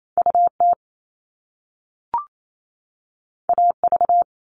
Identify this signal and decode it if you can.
Morse code, heard as a keyed tone, about 28 wpm, 700 Hz. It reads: UT AV
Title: Top Two Letter Combinations in Words 28wpm